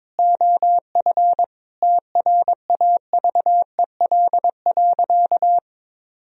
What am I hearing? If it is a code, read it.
Morse code, heard as a keyed tone, about 21 wpm, 700 Hz. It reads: OF TRAVEL.
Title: Sentences from Top 1000 Words - Review Part 4 - 22wpm